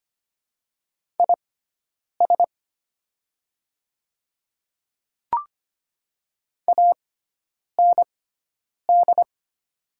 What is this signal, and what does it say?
Morse code, heard as a keyed tone, about 25 wpm, spaced as for 9 wpm, 700 Hz. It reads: IS AND
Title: 6x Character Spacing - Top 100 Words - 25wpm